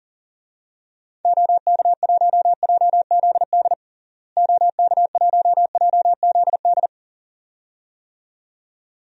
Morse code, heard as a keyed tone, about 40 wpm, 700 Hz. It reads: OK1JZD OK1JZD
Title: DXCC-100 Call Signs - 6 Characters - Repeated 2x - 40wpm